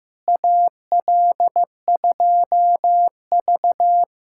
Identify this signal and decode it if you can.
Morse code, heard as a keyed tone, about 15 wpm, 700 Hz. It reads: AL2V
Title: USA Callsign DE Callsign - 4 Characters - 15wpm